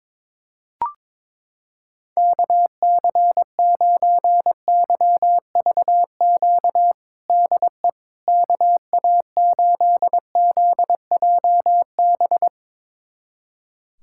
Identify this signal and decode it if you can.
Morse code, heard as a keyed tone, about 22 wpm, 700 Hz. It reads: KC9YVQ DE KA8ZJB
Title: USA Callsign DE Callsign - 6 Characters - 22wpm